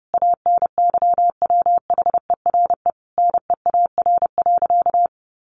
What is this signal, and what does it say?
Morse code, heard as a keyed tone, about 30 wpm, 700 Hz. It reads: ANYWHERE NEAR.
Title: Sentences from Top 1500 Words 30wpm